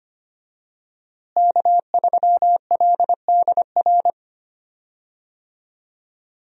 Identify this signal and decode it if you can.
Morse code, heard as a keyed tone, about 25 wpm, 700 Hz. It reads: K3LDR
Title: USA Call Signs - 5 Characters - Repeated 3x - 25wpm